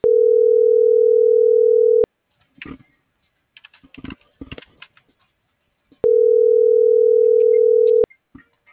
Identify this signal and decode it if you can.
Ringback tone (North America), 440 + 480 Hz.